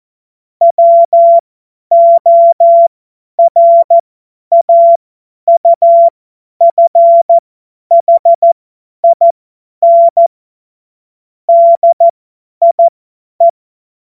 Morse code, heard as a keyed tone, about 14 wpm, spaced as for 10 wpm, 680 Hz. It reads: WORAUFHIN DIE
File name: Morsekurs / Witze / Witz 10. Januar 2023 50 F70